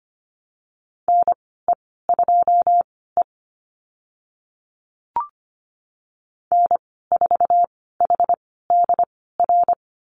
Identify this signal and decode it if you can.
Morse code, heard as a keyed tone, about 25 wpm, spaced as for 16 wpm, 700 Hz. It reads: NE2E N4HDR
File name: Lesson 15g - Number 5 - Call Signs 25wpm